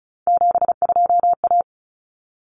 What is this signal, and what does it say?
Morse code, heard as a keyed tone, about 34 wpm, 700 Hz. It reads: 72A
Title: Common Cut Numbers - 3-Digit 35wpm